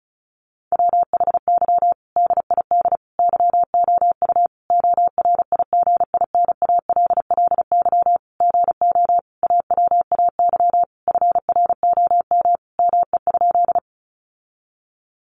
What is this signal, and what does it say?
Morse code, heard as a keyed tone, about 34 wpm, 700 Hz. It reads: WHY DID YOU ORIGINALLY GO AWAY FROM ME?
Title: Sentences from Top 1500 Words 35wpm